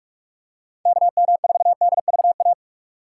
Morse code, heard as a keyed tone, about 47 wpm, 700 Hz. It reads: KM4DVA
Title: USA Call Signs - 4x Character Spacing - 45wpm